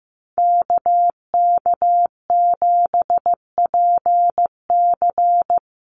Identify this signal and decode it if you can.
Morse code, heard as a keyed tone, about 15 wpm, 700 Hz. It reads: KK7PC
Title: USA Call Signs Repeated 3x 15wpm